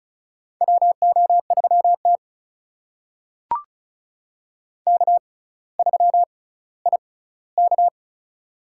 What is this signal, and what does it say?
Morse code, heard as a keyed tone, about 36 wpm, spaced as for 18 wpm, 700 Hz. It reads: WO3T K3IK
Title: USA Call Signs - 4 Characters - 6x Character Spacing - 35wpm